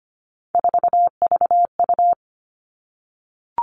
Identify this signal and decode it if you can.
Morse code, heard as a keyed tone, about 25 wpm, 700 Hz. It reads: 4VU
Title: All Cut Numbers - 3-Digit 25wpm